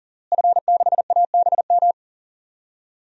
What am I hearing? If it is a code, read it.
Morse code, heard as a keyed tone, about 39 wpm, 700 Hz. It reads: F6ABM
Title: DXCC-100 Call Signs - 5 Characters - Repeated 3x - 40wpm